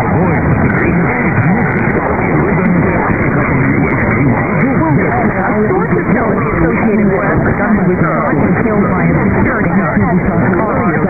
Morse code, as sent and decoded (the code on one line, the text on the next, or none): none